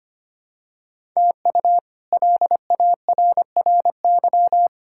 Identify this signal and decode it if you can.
Morse code, heard as a keyed tone, about 25 wpm, 700 Hz. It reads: TU LARRY
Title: K1USN SST QSO Practice - 25wpm